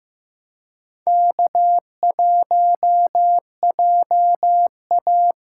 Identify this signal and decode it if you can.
Morse code, heard as a keyed tone, about 15 wpm, 700 Hz. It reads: K1JA
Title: USA Call Signs - 4 Characters - 4x Character Spacing - 15wpm